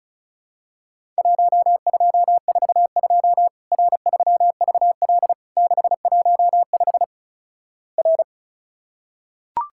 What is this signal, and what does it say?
Morse code, heard as a keyed tone, about 36 wpm, 700 Hz. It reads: 1242 R3VL 615 R
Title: Worked All Europe DX Contest - QTC Exchange - 35wpm